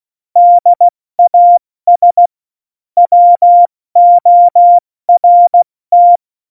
Morse code, heard as a keyed tone, about 16 wpm, 700 Hz. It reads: DAS WORT
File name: Morsekurs / Witze / Witz 26 Mai, 2020 70 F80